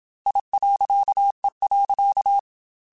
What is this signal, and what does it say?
Morse code, heard as a keyed tone, about 27 wpm, 800 Hz. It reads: I.E.
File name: Technical Subjects / Technical Subjects-mp3s / Reactance @25